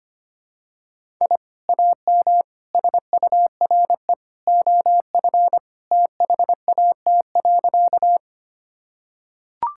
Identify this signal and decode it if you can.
Morse code, heard as a keyed tone, about 25 wpm, 700 Hz. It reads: I AM SURE OF THAT.
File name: practice / 25wpm / Sentences with 2-Word Phrases - 25wpm